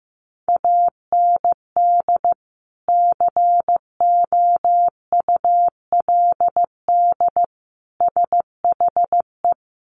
Morse code, heard as a keyed tone, about 14 wpm, 700 Hz. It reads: AND COULD SHE
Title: Sentences from Top 100 Words 15wpm